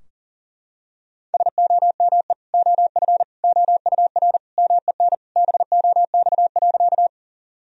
Morse code, heard as a keyed tone, about 37 wpm, 700 Hz. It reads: SOME OF OUR MEN BOX.
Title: Sentences from Top 1000 Words - Review Part 2 - 40wpm